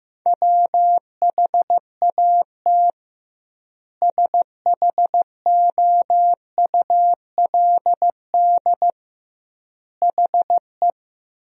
Morse code, heard as a keyed tone, about 15 wpm, 700 Hz. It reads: WHAT SHOULD HE
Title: 2x Word-Spacing - Sentences from Top 300 Words - 15wpm